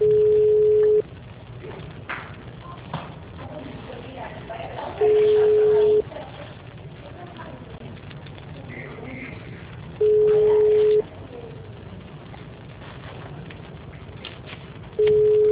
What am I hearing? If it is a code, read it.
Ringback tone (Europe, 425 Hz), with speech.